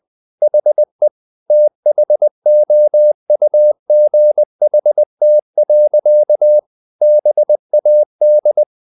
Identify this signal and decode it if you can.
Morse code, heard as a keyed tone, about 20 wpm, 600 Hz. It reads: HE THOUGHT. BAD
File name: the old man and the sea part 2 20wpm